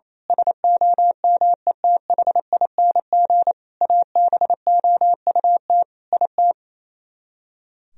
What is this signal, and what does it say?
Morse code, heard as a keyed tone, about 27 wpm, 700 Hz. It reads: SOMETHING ABOUT IT
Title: Speed-Racing - Sets of 5 Words 28wpm